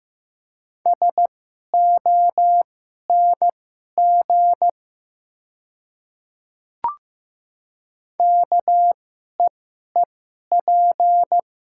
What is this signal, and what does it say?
Morse code, heard as a keyed tone, about 15 wpm, spaced as for 11 wpm, 690 Hz. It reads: SONG KEEP
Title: 2x Character Spacing - Top 500 Words - 15wpm